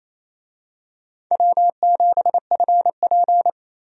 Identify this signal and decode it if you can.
Morse code, heard as a keyed tone, about 28 wpm, 700 Hz. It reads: W7FP
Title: USA Call Signs - 4 Characters - Repeated 3x - 28wpm